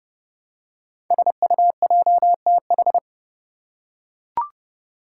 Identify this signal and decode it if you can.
Morse code, heard as a keyed tone, about 30 wpm, 700 Hz. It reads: SUJTH
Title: Code Groups - 5 Letter:Numbers - 30wpm